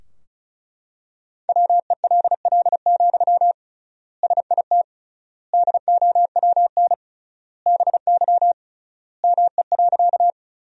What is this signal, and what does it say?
Morse code, heard as a keyed tone, about 36 wpm, 700 Hz. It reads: WELL, SIT DOWN BY ME.
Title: 3x Word-Spacing - Sentences from Top 500 Words - 35wpm